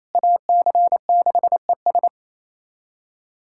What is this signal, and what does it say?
Morse code, heard as a keyed tone, about 28 wpm, 700 Hz. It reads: AC6ES
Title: USA Callsign DE Callsign - 5 Characters - 28wpm